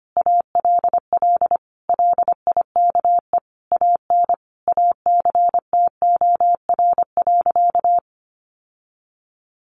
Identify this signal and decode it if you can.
Morse code, heard as a keyed tone, about 25 wpm, 700 Hz. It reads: ALL LIKE AN ACTOR.
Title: Sentences from Top 1500 Words 25wpm